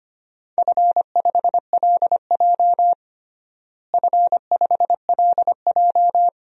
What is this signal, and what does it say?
Morse code, heard as a keyed tone, about 25 wpm, 700 Hz. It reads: F5LJ F5LJ
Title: DXCC-100 Call Signs - 4 Characters - Repeated 2x - 25wpm